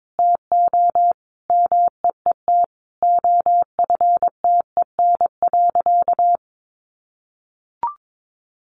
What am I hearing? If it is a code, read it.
Morse code, heard as a keyed tone, about 22 wpm, 700 Hz. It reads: TO MEET OFTEN.
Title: Sentences from Top 1100 Words 22wpm